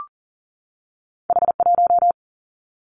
Morse code, heard as a keyed tone, about 41 wpm, 700 Hz. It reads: H 1
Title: Two Challenging Letter-Number Combinations 40wpm